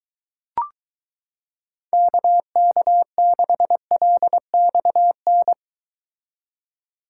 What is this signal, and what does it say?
Morse code, heard as a keyed tone, about 23 wpm, 700 Hz. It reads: KK6LXN